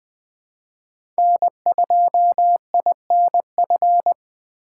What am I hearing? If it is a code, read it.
Morse code, heard as a keyed tone, about 20 wpm, 700 Hz. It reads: N2INF
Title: USA Call Signs - 5 Characters - 2x Character Spacing - 20wpm